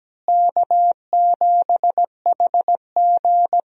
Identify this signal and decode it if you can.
Morse code, heard as a keyed tone, about 17 wpm, 700 Hz. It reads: K7HG
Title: USA Call Signs - 4 Characters - Repeated 2x - 17wpm